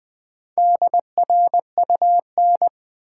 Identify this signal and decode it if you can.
Morse code, heard as a keyed tone, about 20 wpm, 700 Hz. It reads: DRUN
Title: Build A Word - 1 Letter At A Time - 5-Letter Words - 20wpm